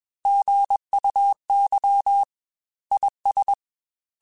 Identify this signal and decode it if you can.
Morse code, heard as a keyed tone, about 21 wpm, 800 Hz. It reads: GUY IS
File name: Chuckles / Chuckles-mp3s / HumanErr @20